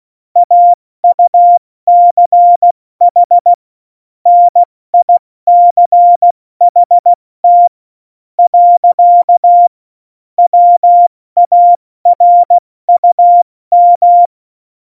Morse code, heard as a keyed tone, about 16 wpm, 700 Hz. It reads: AUCH NICHT . WARUM
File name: Morsekurs / Witze / Witz 28. August 2018, 70 F80